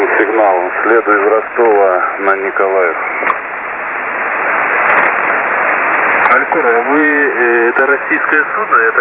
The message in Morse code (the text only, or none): none